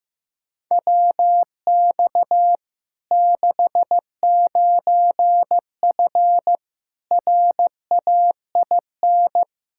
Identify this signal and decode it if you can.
Morse code, heard as a keyed tone, about 15 wpm, 700 Hz. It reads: WX 69F RAIN
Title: Practice QSO - Exchange 2 - Info - 15wpm